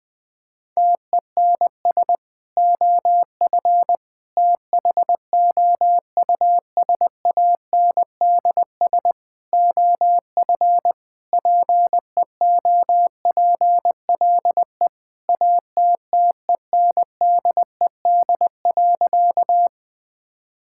Morse code, heard as a keyed tone, about 20 wpm, 700 Hz. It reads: TENS OF THOUSANDS OF PEOPLE ATTENDED.
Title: Sentences with 4-Word Phrases - 20wpm